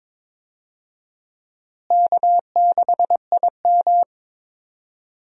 Morse code, heard as a keyed tone, about 22 wpm, 700 Hz. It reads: K6IM
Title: USA Call Signs - 4 Characters - Repeated 3x - 22wpm